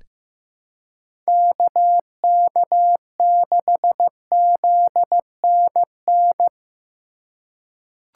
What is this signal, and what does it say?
Morse code, heard as a keyed tone, about 15 wpm, 700 Hz. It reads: KK6ZNN
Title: USA Call Signs - 6 Characters - 6x Character Spacing - 15wpm